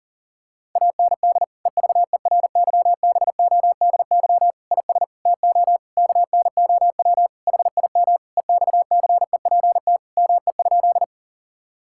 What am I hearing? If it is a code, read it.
Morse code, heard as a keyed tone, about 40 wpm, 700 Hz. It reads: AND EVERYBODY IS TO KNOW HIM EXCEPT ME?
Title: Sentences from Top 1600 Words 40wpm